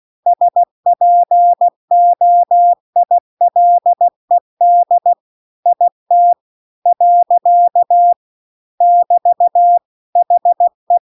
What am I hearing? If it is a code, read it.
Morse code, heard as a keyed tone, about 16 wpm, 700 Hz. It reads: SPOILED IT . <BT> HE
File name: CW-The-Big-Twoheaded-River-2-16WPM